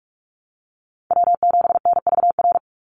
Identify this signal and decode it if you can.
Morse code, heard as a keyed tone, about 46 wpm, 690 Hz. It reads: W7NVR
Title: USA Call Signs 45wpm